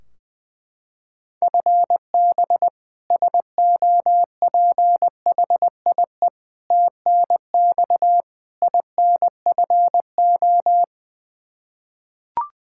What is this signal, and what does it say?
Morse code, heard as a keyed tone, about 20 wpm, 700 Hz. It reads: FB SOPHIE TNX INFO
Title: K1USN SST QSO Practice 20wpm